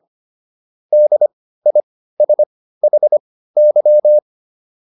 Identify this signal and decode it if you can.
Morse code, heard as a keyed tone, about 25 wpm, spaced as for 15 wpm, 600 Hz. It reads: DISHY